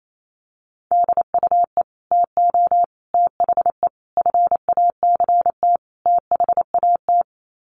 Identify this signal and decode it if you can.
Morse code, heard as a keyed tone, about 28 wpm, 690 Hz. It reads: DUE TO THE FACT THAT
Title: Sets of 5 Words Encore 28wpm